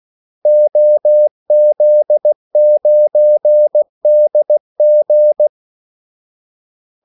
Morse code, heard as a keyed tone, about 16 wpm, 600 Hz. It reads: OZ9DG
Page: Kaldesignaler 16 wpm